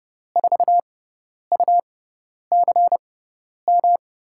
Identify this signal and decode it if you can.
Morse code, heard as a keyed tone, about 30 wpm, spaced as for 10 wpm, 700 Hz. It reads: 4UCM
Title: USA Call Signs - 6x Character Spacing - 30wpm